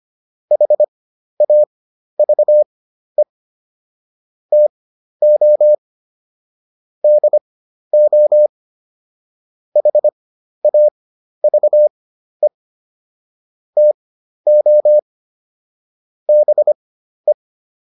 Morse code, heard as a keyed tone, about 25 wpm, spaced as for 12 wpm, 600 Hz. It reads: HAVE TO DO HAVE TO BE